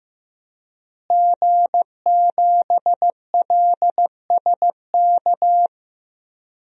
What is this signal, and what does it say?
Morse code, heard as a keyed tone, about 15 wpm, 700 Hz. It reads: G7LSK
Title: DXCC-100 Call Signs - 5 Characters - Repeated 2x - 15wpm